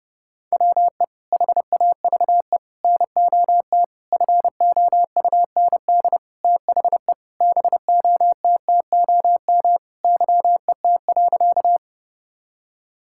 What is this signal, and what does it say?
Morse code, heard as a keyed tone, about 30 wpm, 700 Hz. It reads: WE HAVE NOT FOUND THE BOTTOM YET.
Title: Sentences from Top 1000 Words - Review Part 6 - 30wpm